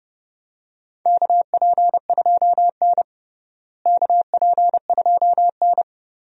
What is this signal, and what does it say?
Morse code, heard as a keyed tone, about 30 wpm, 700 Hz. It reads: KP2N KP2N
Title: DXCC-100 Call Signs - 4 Characters - Repeated 3x - 30wpm